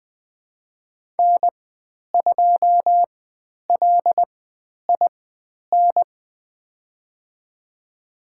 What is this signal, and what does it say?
Morse code, heard as a keyed tone, about 20 wpm, spaced as for 10 wpm, 700 Hz. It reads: N2LIN
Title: 095 Lesson 16g - Letter c - Call Signs 20wpm